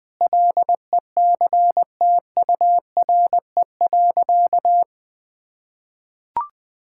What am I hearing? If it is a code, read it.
Morse code, heard as a keyed tone, about 21 wpm, 700 Hz. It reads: LECTURE.
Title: Sentences from Top 1600 Words 20wpm